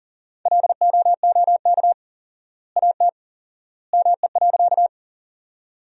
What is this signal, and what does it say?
Morse code, heard as a keyed tone, about 40 wpm, 700 Hz. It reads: LOOK AT ME.
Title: 4x Word-Spacing - Sentences from Top 200 Words - 40wpm